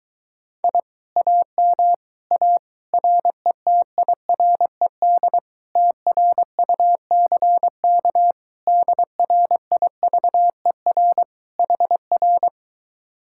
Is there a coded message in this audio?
Yes: Morse code from 0.6 to 12.5 s, heard as a keyed tone, about 22 wpm, 700 Hz. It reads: I AM A RETIRED TRUCK DRIVER HR